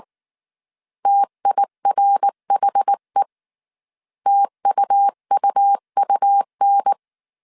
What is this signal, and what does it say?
Morse code, heard as a keyed tone, about 19 wpm, 800 Hz. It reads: TIRHE TUUUN
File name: CW / DET / 1 3 1500C S B 18wpm 1916